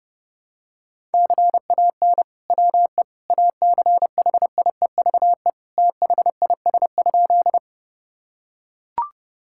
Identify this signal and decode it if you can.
Morse code, heard as a keyed tone, about 30 wpm, 700 Hz. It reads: CAN WE ACHIEVE THIS?
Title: Sentences from Top 1000 Words - Review Part 6 - 30wpm